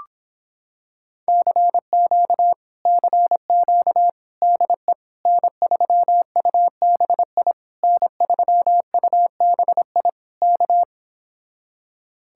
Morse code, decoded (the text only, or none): CQ CQ DE N3UBI N3UBI K